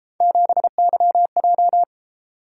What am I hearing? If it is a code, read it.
Morse code, heard as a keyed tone, about 31 wpm, 700 Hz. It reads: 7YJ